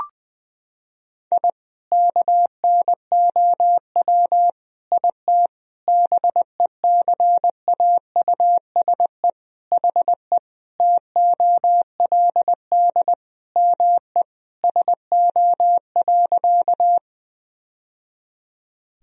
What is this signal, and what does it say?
Morse code, heard as a keyed tone, about 21 wpm, 700 Hz. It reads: I KNOW IT BECAUSE HE TOLD ME SO.
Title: Sentences from Top 1000 Words - Review Part 3 - 20wpm